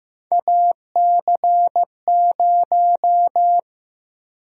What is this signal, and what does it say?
Morse code, heard as a keyed tone, about 15 wpm, 700 Hz. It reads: AC0
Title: USA Call Sign Prefixes - 15wpm